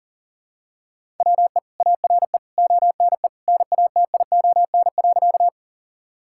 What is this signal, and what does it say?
Morse code, heard as a keyed tone, about 37 wpm, 700 Hz. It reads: WE ARE ONE NATION.
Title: Sentences from Top 1000 Words - Review Part 6 - 40wpm